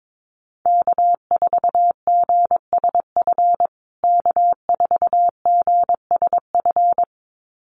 Morse code, heard as a keyed tone, about 22 wpm, 700 Hz. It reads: K4GSF K4GSF
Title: USA Call Signs Repeated 2x 22wpm